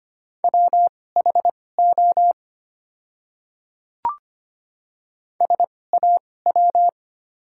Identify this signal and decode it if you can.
Morse code, heard as a keyed tone, about 25 wpm, spaced as for 18 wpm, 700 Hz. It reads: WHO SAW
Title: 2x Character Spacing - Top 500 Words - 25wpm